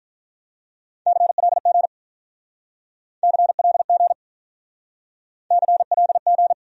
Morse code, heard as a keyed tone, about 47 wpm, 700 Hz. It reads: CLG CLG CLG